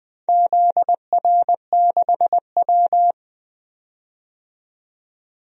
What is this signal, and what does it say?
Morse code, heard as a keyed tone, about 21 wpm, 700 Hz. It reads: ZR6W
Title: DXCC-100 Call Signs - 4 Characters - 20wpm